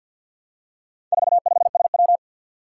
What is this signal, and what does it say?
Morse code, heard as a keyed tone, about 51 wpm, 700 Hz. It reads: 45SW
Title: Four Challenging Letter-Number Combinations 50wpm